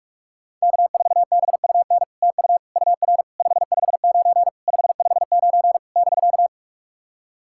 Morse code, heard as a keyed tone, about 44 wpm, 700 Hz. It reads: K4BUN TU UR 559 559 <BK>
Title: SOTA QSO Practice 45wpm